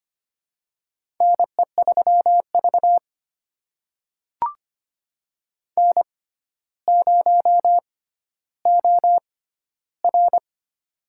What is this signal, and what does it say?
Morse code, heard as a keyed tone, about 25 wpm, spaced as for 13 wpm, 700 Hz. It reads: NE3V N0OR
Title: USA Call Signs - 4 Characters - 6x Character Spacing - 25wpm